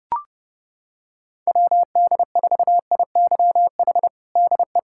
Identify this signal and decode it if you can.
Morse code, heard as a keyed tone, about 30 wpm, 700 Hz. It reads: WD4IYH DE
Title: Practice QSO - Exchange 4 - Ending - 30wpm